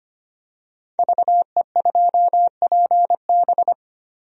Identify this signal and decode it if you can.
Morse code, heard as a keyed tone, about 25 wpm, 700 Hz. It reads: VE2PB
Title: DXCC-100 Call Signs - 5 Characters - 25wpm